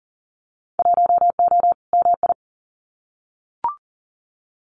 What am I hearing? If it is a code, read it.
Morse code, heard as a keyed tone, about 40 wpm, 700 Hz. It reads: 1O MI
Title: Winter Field Day Exchanges 40wpm